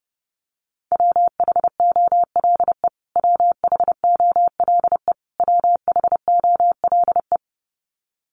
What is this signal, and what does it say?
Morse code, heard as a keyed tone, about 30 wpm, 700 Hz. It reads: WHOLE WHOLE WHOLE
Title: Top 500 Words Repeated 30wpm